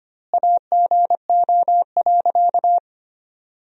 Morse code, heard as a keyed tone, about 25 wpm, 700 Hz. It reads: AGO.
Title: 2x Word-Spacing - Sentences from Top 500 Words - 25wpm